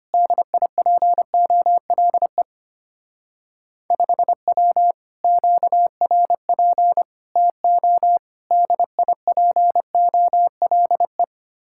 Morse code, heard as a keyed tone, about 32 wpm, 700 Hz. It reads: DIPOLE 5W QRP TO DIPOLE